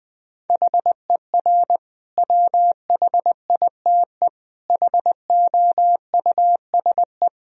Morse code, heard as a keyed tone, about 20 wpm, 700 Hz. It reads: HER WHITE HOUSE
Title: Sentences from Top 1000 Words - Review Part 1 - 20wpm